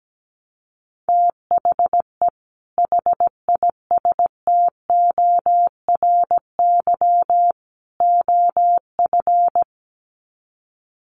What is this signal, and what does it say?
Morse code, heard as a keyed tone, about 17 wpm, 700 Hz. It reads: THE HISTORY OF
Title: Sets of 3 Words 17wpm